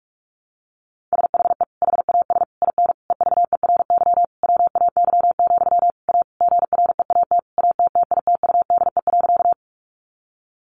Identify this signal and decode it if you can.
Morse code, heard as a keyed tone, about 44 wpm, 700 Hz. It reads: SHE HAS IN EVERY WAY, A GREAT ATTITUDE.
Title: Sentences from Top 1300 Words 45wpm